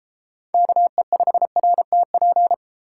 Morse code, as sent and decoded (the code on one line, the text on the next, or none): -.- . ..... .-. - .--.
KE5RTP